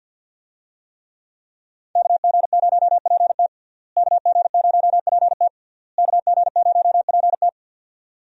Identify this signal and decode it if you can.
Morse code, heard as a keyed tone, about 52 wpm, 700 Hz. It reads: KG0PT KG0PT KG0PT
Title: USA Call Signs - 5 Characters - Repeated 3x - 50wpm